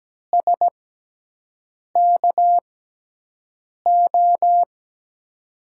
Morse code, heard as a keyed tone, about 17 wpm, spaced as for 6 wpm, 700 Hz. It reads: SKO
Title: USA Call Signs - 6x Character Spacing - 17wpm